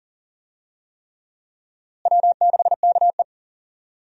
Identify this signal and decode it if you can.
Morse code, heard as a keyed tone, about 40 wpm, 700 Hz. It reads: W6KE
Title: USA Call Signs - 4 Characters - Repeated 3x - 40wpm